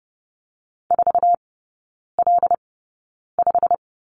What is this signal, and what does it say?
Morse code, heard as a keyed tone, about 30 wpm, spaced as for 9 wpm, 700 Hz. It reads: 4L5
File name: course / 30:10wpm / 087 Lesson 15e - Number 5 - Three Characters 30wpm